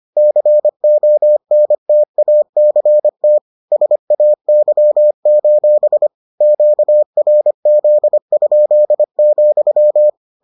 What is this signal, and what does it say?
Morse code, heard as a keyed tone, about 25 wpm, 600 Hz. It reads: CONTACT SAY: QRZ?,